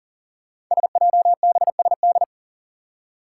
Morse code, heard as a keyed tone, about 40 wpm, 700 Hz. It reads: SJBSD
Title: Five Challenging Letter-Number Combinations 40wpm